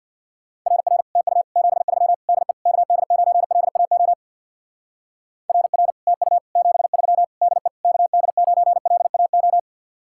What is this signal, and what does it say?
Morse code, heard as a keyed tone, about 64 wpm, 700 Hz. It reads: RR TU 73 DE KD9LAO RR TU 73 DE KD9LAO